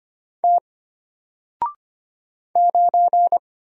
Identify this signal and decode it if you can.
Morse code, heard as a keyed tone, about 25 wpm, 700 Hz. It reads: T 9
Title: Single Character (Letter:Number:Punctuation:Prosign) Alt 25wpm